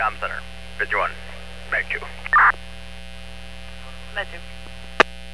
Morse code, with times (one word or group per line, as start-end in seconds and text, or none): none